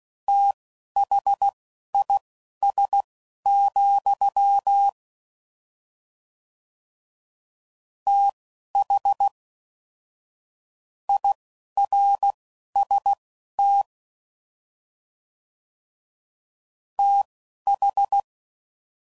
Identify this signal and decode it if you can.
Morse code, heard as a keyed tone, about 16 wpm, spaced as for 11 wpm, 800 Hz. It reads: THIS, TH IRST TH
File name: Prefixes / FLs & THs / Fast TH Prefixes